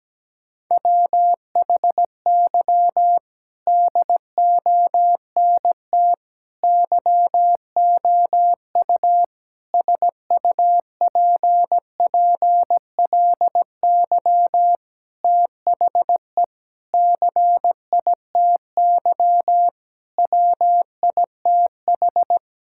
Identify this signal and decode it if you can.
Morse code, heard as a keyed tone, about 17 wpm, 700 Hz. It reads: WHY DONT YOU SUPPLY THE CITY WITH
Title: Sentences from Top 1000 Words - Review Part 5 - 17wpm